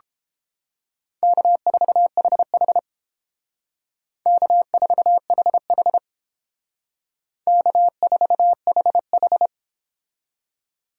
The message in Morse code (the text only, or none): K4HH K4HH K4HH